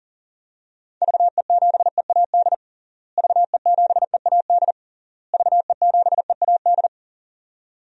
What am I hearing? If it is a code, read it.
Morse code, heard as a keyed tone, about 40 wpm, 700 Hz. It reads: VE7EAD VE7EAD VE7EAD
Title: DXCC-100 Call Signs - 6 Characters - Repeated 3x - 40wpm